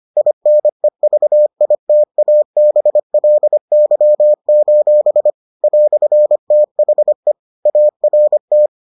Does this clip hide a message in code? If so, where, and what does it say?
Yes: Morse code from 0.2 to 8.7 s, heard as a keyed tone, about 25 wpm, 600 Hz. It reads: INEVITABLY: "THE ART